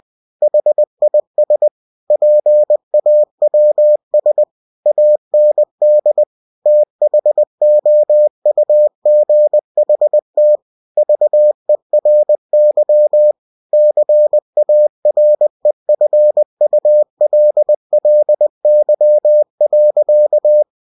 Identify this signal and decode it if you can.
Morse code, heard as a keyed tone, about 20 wpm, 600 Hz. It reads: HIS PAWS AND THOUGHT VERY CAREFULLY.
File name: cw-books / winnie-the-pooh / winnie-20wpm-page-0005